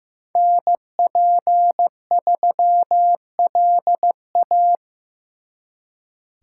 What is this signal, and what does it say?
Morse code, heard as a keyed tone, about 18 wpm, 700 Hz. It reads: NP3LA
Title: Speed Racing - USA Call Signs 15wpm